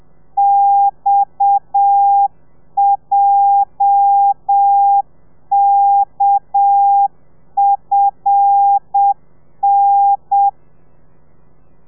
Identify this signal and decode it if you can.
Morse code, heard as a keyed tone, about 7 wpm, 800 Hz. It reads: XJKFN
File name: telegrafia / i2tfj / Lez8 30